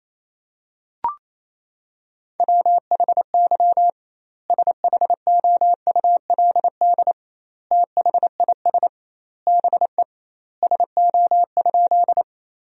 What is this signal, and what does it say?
Morse code, heard as a keyed tone, about 28 wpm, 700 Hz. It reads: WHY SHOULD THIS BE SO?
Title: 2x Word-Spacing - Sentences from Top 300 Words - 28wpm